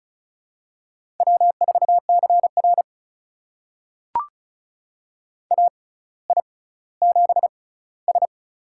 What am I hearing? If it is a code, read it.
Morse code, heard as a keyed tone, about 36 wpm, spaced as for 18 wpm, 700 Hz. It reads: W4CR AI7S
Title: USA Call Signs - 4 Characters - 6x Character Spacing - 35wpm